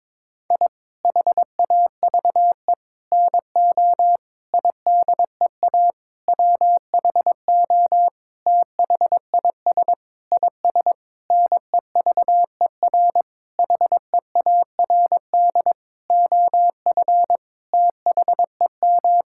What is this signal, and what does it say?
Morse code, heard as a keyed tone, about 22 wpm, 700 Hz. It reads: I HAVE NO IDEA WHO THIS IS NEVER HEARD OF THEM
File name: Sentences with 9-Word Phrases - 22wpm